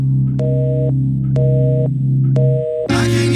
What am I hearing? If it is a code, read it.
Busy tone (North America), 480 + 620 Hz.